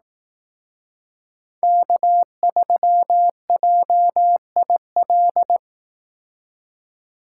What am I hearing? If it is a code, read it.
Morse code, heard as a keyed tone, about 18 wpm, 700 Hz. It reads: K3JIL
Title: Speed Racing - USA 1x3 Call Signs 15wpm